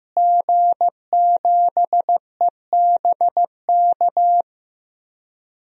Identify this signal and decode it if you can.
Morse code, heard as a keyed tone, about 15 wpm, 700 Hz. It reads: G7EBK
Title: DXCC-100 Call Signs - 5 Characters - Repeated 2x - 15wpm